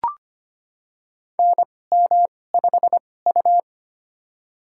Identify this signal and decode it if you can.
Morse code, heard as a keyed tone, about 25 wpm, spaced as for 18 wpm, 700 Hz. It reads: NM5U
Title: USA Call Signs - 4 Characters - 2x Character Spacing - 25wpm